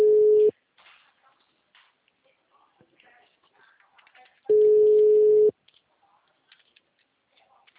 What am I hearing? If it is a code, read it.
Ringback tone (Europe, 425 Hz).
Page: Seu Credito